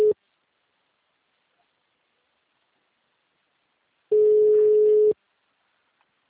Ringback tone (Europe, 425 Hz).